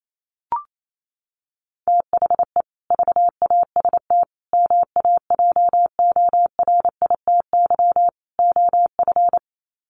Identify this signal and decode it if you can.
Morse code, heard as a keyed tone, about 28 wpm, 700 Hz. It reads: THE VAST MAJORITY OF
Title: Sets of 4 Words 28wpm